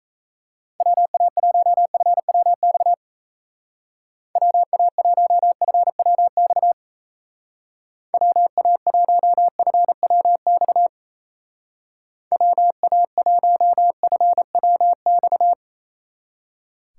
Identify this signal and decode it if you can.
Morse code, heard as a keyed tone, about 37 wpm, 700 Hz. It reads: WA1FWX WA1FWX WA1FWX WA1FWX